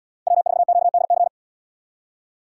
Morse code, heard as a keyed tone, about 77 wpm, 700 Hz. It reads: W4QDC